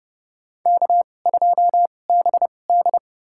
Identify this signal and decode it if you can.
Morse code, heard as a keyed tone, about 30 wpm, spaced as for 22 wpm, 700 Hz. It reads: K2BD
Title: USA Call Signs - 4 Characters - 2x Character Spacing - 30wpm